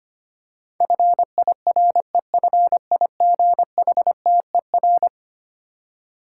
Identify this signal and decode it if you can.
Morse code, heard as a keyed tone, about 25 wpm, 700 Hz. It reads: FIREFIGHTER
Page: morse-code-ninja-practice-25wpm